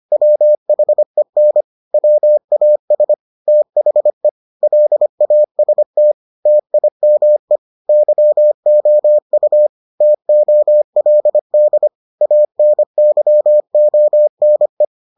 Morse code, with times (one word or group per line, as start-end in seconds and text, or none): 0.1-1.6 WHEN
1.9-3.1 WAS
3.5-4.3 THE
4.6-6.1 LAST
6.5-7.6 TIME
7.9-9.7 YOU
10.0-11.9 TOLD
12.2-14.8 ANYONE